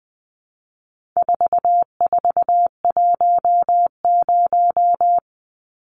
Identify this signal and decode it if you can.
Morse code, heard as a keyed tone, about 20 wpm, 700 Hz. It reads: 4410
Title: Numbers - 4-Digit 20wpm